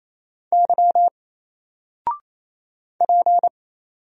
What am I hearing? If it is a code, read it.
Morse code, heard as a keyed tone, about 28 wpm, 700 Hz. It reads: Y P
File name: Single-Letter-Number-28wpm